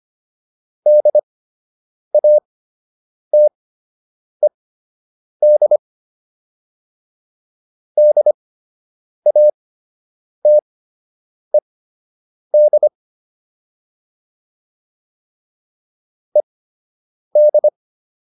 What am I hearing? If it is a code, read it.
Morse code, heard as a keyed tone, about 25 wpm, spaced as for 8 wpm, 600 Hz. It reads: DATED DATED ED